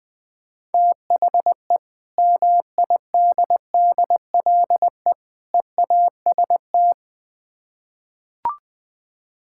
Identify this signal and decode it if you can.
Morse code, heard as a keyed tone, about 20 wpm, 700 Hz. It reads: THE MIDDLE EAST
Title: Build A Phrase - 3-Word Phrases - 20wpm